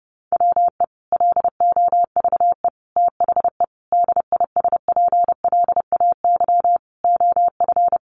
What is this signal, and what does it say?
Morse code, heard as a keyed tone, about 30 wpm, 700 Hz. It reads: WE LOVE THE DISPLAY OF
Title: Sentences from Top 1200 Words 30wpm